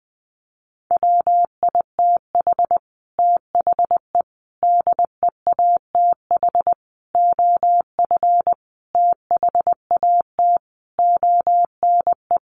Morse code, heard as a keyed tone, about 20 wpm, 700 Hz. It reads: WITH THE DEATH OF THAT ONE
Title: A Sound Of Thunder 20wpm